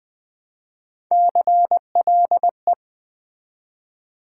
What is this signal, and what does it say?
Morse code, heard as a keyed tone, about 20 wpm, 700 Hz. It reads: CLE
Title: Build A Word - 1 Letter At A Time - 5-Letter Words - 20wpm